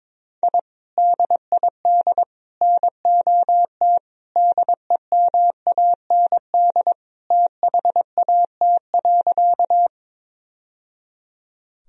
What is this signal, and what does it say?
Morse code, heard as a keyed tone, about 22 wpm, 700 Hz. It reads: I DID NOT DEMAND THAT.
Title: Sentences from Top 1000 Words - Review Part 4 - 22wpm